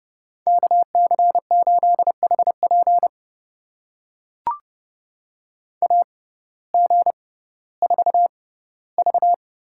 Morse code, heard as a keyed tone, about 31 wpm, spaced as for 17 wpm, 700 Hz. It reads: KC8HP AG4V
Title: USA Call Signs - 5 Characters - 6x Character Spacing - 30wpm